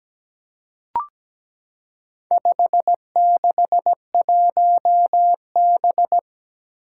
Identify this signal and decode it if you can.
Morse code, heard as a keyed tone, about 17 wpm, 700 Hz. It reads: 561B
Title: Four Challenging Letter-Number Combinations 17wpm